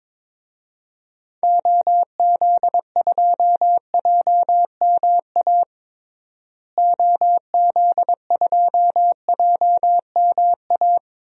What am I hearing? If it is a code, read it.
Morse code, heard as a keyed tone, about 22 wpm, 700 Hz. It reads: OZ2JMA OZ2JMA
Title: DXCC-100 Call Signs - 6 Characters - Repeated 3x - 22wpm